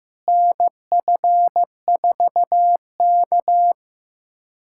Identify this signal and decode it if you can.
Morse code, heard as a keyed tone, about 15 wpm, 700 Hz. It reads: NF4K
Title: USA Callsign DE Callsign - 4 Characters - 15wpm